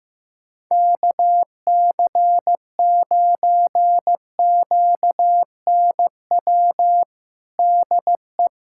Morse code, heard as a keyed tone, about 16 wpm, 700 Hz. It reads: KC9QNW DE
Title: USA Callsign DE Callsign 15wpm